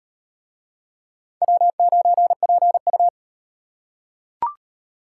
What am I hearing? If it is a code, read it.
Morse code, heard as a keyed tone, about 36 wpm, 700 Hz. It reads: W9PU